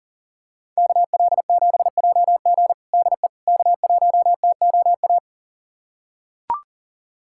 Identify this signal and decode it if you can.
Morse code, heard as a keyed tone, about 40 wpm, 700 Hz. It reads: KL7JG DE K1TOA
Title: USA Callsign DE Callsign - 5 Characters - 40wpm